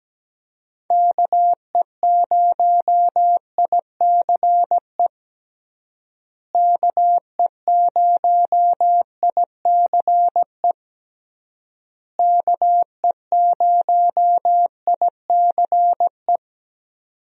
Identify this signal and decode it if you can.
Morse code, heard as a keyed tone, about 17 wpm, 700 Hz. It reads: KE0ICE KE0ICE KE0ICE
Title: USA Call Signs - 6 Characters - Repeated 3x - 17wpm